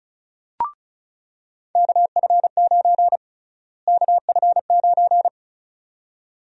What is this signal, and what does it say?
Morse code, heard as a keyed tone, about 36 wpm, 700 Hz. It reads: KF9 KF9
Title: USA Call Sign Prefixes - Repeated 2x - 35wpm